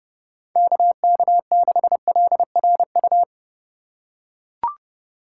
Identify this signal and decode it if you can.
Morse code, heard as a keyed tone, about 30 wpm, 700 Hz. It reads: KK6LRU
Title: USA Call Signs - 6 Characters - 3x Character Spacing - 30wpm